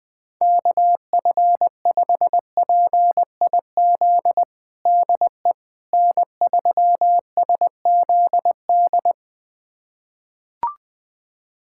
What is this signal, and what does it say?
Morse code, heard as a keyed tone, about 21 wpm, 700 Hz. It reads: KF5PIZ DE N3SZD
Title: USA Callsign DE Callsign 20wpm